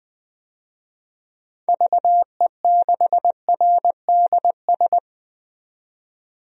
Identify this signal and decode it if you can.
Morse code, heard as a keyed tone, about 20 wpm, 700 Hz. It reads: VE6RDS
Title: DXCC-100 Call Signs - 6 Characters - 20wpm